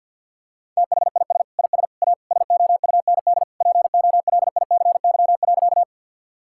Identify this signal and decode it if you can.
Morse code, heard as a keyed tone, about 50 wpm, 700 Hz. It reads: THIS IS A SOUND POLICY.
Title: Sentences from Top 1000 Words - Review Part 5 - 50wpm